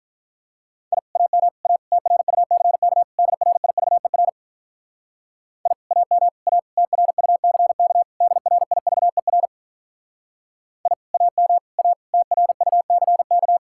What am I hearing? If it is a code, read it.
Morse code, heard as a keyed tone, about 47 wpm, 700 Hz. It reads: I AM A TRUCK DRIVER I AM A TRUCK DRIVER I AM A TRUCK